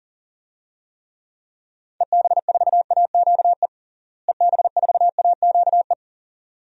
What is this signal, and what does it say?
Morse code, heard as a keyed tone, about 37 wpm, 700 Hz. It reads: EB4AQE EB4AQE
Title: DXCC-100 Call Signs - 6 Characters - Repeated 2x - 40wpm